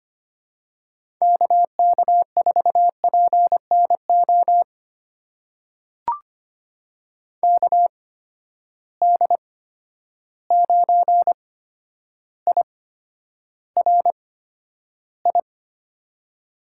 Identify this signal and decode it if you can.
Morse code, heard as a keyed tone, about 25 wpm, spaced as for 11 wpm, 700 Hz. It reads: KK4PNO KD9IRI
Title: USA Call Signs - 6 Characters - 8x Character Spacing - 25wpm